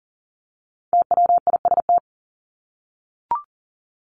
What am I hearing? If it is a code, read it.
Morse code, heard as a keyed tone, about 40 wpm, 700 Hz. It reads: TWIST
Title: Top 1751-2000 Words 40wpm